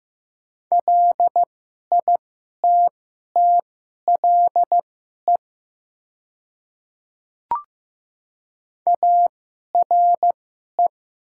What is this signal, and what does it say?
Morse code, heard as a keyed tone, about 15 wpm, spaced as for 11 wpm, 700 Hz. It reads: LITTLE ARE
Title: 2x Character Spacing - Top 200 Words - 15wpm